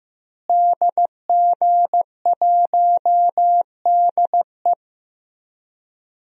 Morse code, heard as a keyed tone, about 15 wpm, 700 Hz. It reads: DG1DE
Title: DXCC-100 Call Signs - 5 Characters - Repeated 3x - 15wpm